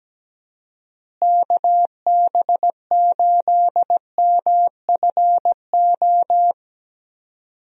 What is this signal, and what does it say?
Morse code, heard as a keyed tone, about 17 wpm, 700 Hz. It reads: KB8MFO
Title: USA Call Signs - 6 Characters - Repeated 2x - 17wpm